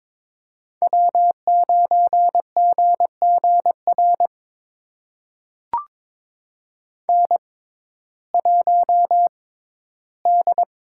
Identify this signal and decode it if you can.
Morse code, heard as a keyed tone, about 22 wpm, spaced as for 14 wpm, 700 Hz. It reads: W9GGR N1D
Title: USA Call Signs - 5 Characters - 6x Character Spacing - 22wpm